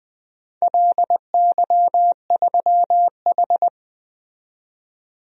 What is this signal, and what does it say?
Morse code, heard as a keyed tone, about 20 wpm, 700 Hz. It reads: LY3H
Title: DXCC-100 Call Signs - 4 Characters - 20wpm